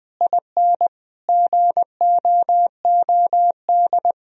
Morse code, heard as a keyed tone, about 21 wpm, 700 Hz. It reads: IN GOOD
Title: Sentences from Top 1000 Words - Review Part 3 - 20wpm